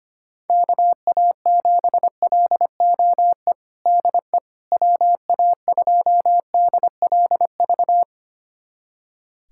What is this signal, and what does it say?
Morse code, heard as a keyed tone, about 25 wpm, 700 Hz. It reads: KA7LOE DE WA2DLV
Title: USA Callsign DE Callsign - 6 Characters - 25wpm